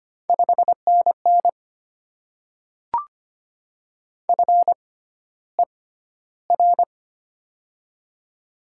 Morse code, heard as a keyed tone, about 25 wpm, spaced as for 13 wpm, 700 Hz. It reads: 5NN FER
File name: Top 20 QSO Elements - 6x Character Spacing - 25wpm